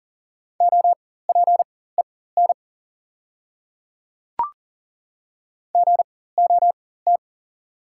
Morse code, heard as a keyed tone, about 40 wpm, spaced as for 19 wpm, 690 Hz. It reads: OPEN GOT
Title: 4x Character Spacing - Top 300 Words - 40wpm